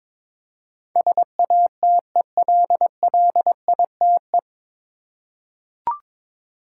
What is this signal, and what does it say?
Morse code, heard as a keyed tone, about 22 wpm, 700 Hz. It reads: SATELLITE
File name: 9 Letter Words - First 3 Letters Given - 22wpm